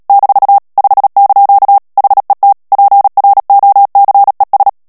Morse code, heard as a keyed tone, about 37 wpm, 800 Hz. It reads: - 5) HET PROCES
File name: morse mp3 / HOLLANDAIS texte clair 2 vitesse 35 mots minute